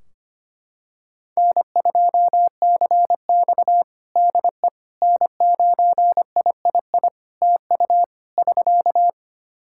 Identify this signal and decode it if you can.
Morse code, heard as a keyed tone, about 25 wpm, 700 Hz. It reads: N2CX DE N9III TU <SK>
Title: K1USN SST QSO Practice 25wpm